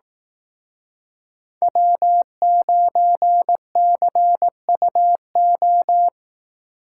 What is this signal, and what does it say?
Morse code, heard as a keyed tone, about 18 wpm, 700 Hz. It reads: W9CUO
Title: Speed Racing - USA 1x3 Call Signs 15wpm